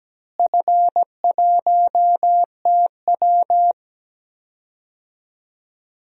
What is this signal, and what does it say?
Morse code, heard as a keyed tone, about 17 wpm, 700 Hz. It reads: F1TW
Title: DXCC-100 Call Signs - 4 Characters - Repeated 2x - 17wpm